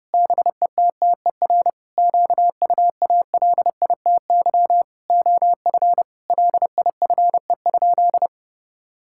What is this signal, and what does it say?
Morse code, heard as a keyed tone, about 30 wpm, 700 Hz. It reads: BETTER QUALITY OF LIFE?
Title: Sentences from Top 1000 Words - Review Part 5 - 30wpm